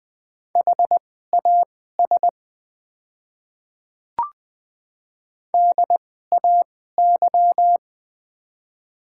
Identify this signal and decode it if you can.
Morse code, heard as a keyed tone, about 20 wpm, spaced as for 14 wpm, 700 Hz. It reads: HAS DAY
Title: 2x Character Spacing - Top 100 Words - 20wpm